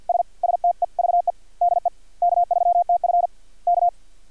Morse code, heard as a keyed tone, about 50 wpm, 700 Hz. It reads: S STEVE DE K3TF K
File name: archived / CRSnet